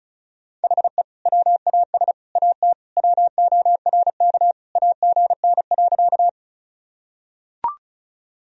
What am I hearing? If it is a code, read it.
Morse code, heard as a keyed tone, about 35 wpm, 700 Hz. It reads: HE WAS AT WORK AGN.
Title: Sentences from Top 200 Words and Abbreviations 35wpm